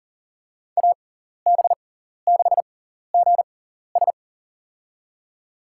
Morse code, heard as a keyed tone, about 40 wpm, spaced as for 14 wpm, 700 Hz. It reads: AB6GS
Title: USA Call Signs - 6x Character Spacing - 40wpm